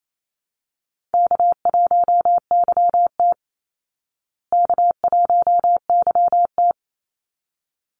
Morse code, heard as a keyed tone, about 28 wpm, 700 Hz. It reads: K1YT K1YT
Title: Easy Call Signs Repeated 2x - 28wpm